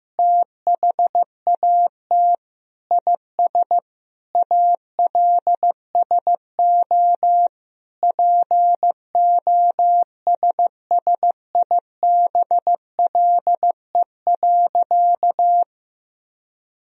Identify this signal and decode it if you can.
Morse code, heard as a keyed tone, about 15 wpm, 700 Hz. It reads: THAT IS ALSO POSSIBLE.
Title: Sentences from Top 1000 Words - Review Part 1 - 15wpm